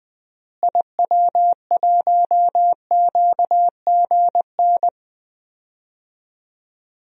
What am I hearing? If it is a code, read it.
Morse code, heard as a keyed tone, about 20 wpm, 700 Hz. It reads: IW1QGN
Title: DXCC-100 Call Signs - 6 Characters - Repeated 2x - 20wpm